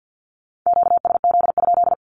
Morse code, heard as a keyed tone, about 50 wpm, 700 Hz. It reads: QSZ?
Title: Q-Codes Repeated 3x 50wpm